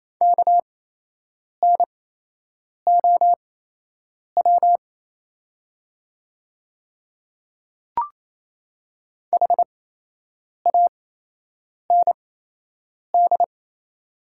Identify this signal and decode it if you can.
Morse code, heard as a keyed tone, about 28 wpm, spaced as for 8 wpm, 700 Hz. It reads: KNOW HAND
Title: 8x Character Spacing - Top 200 Words - 28wpm